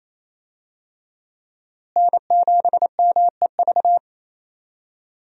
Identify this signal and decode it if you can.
Morse code, heard as a keyed tone, about 28 wpm, 700 Hz. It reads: N7MEV
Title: USA Call Signs - 5 Characters - Repeated 2x - 28wpm